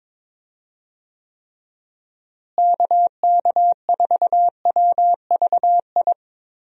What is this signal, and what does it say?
Morse code, heard as a keyed tone, about 22 wpm, 700 Hz. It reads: KK4WVI
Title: USA Call Signs - 6 Characters - Repeated 3x - 22wpm